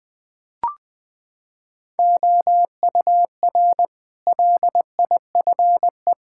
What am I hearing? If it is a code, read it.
Morse code, heard as a keyed tone, about 20 wpm, 700 Hz. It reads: OUR LIFE
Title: Sentences from Top 1000 Words - Review Part 5 - 20wpm